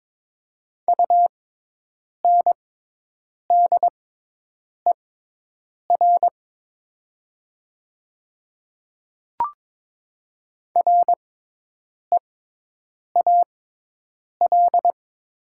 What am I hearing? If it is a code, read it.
Morse code, heard as a keyed tone, about 22 wpm, spaced as for 8 wpm, 700 Hz. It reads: UNDER REAL
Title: 6x Character Spacing - Top 500 Words - 22wpm